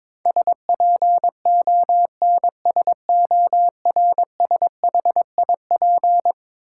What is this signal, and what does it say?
Morse code, heard as a keyed tone, about 22 wpm, 700 Hz. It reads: SPONSORSHIP
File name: practice / 22wpm / Eleven Letter Words - 22wpm